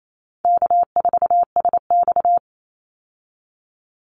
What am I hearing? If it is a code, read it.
Morse code, heard as a keyed tone, about 28 wpm, 700 Hz. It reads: K4SX
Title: Call Signs Repeated 2x - 28wpm